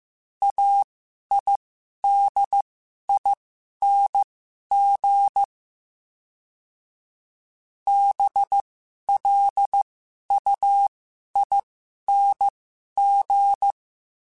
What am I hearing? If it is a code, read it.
Morse code, heard as a keyed tone, about 15 wpm, spaced as for 11 wpm, 800 Hz. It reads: AIDING BLUING
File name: Special INGs-mp3s / Special INGs-mp3s / Sngl Ing @10